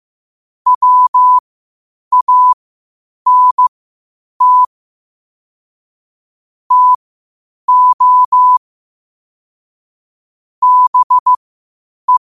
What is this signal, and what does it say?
Morse code, heard as a keyed tone, about 15 wpm, spaced as for 8 wpm, 1000 Hz. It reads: WANT TO BE
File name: qotdfiles / 8wpm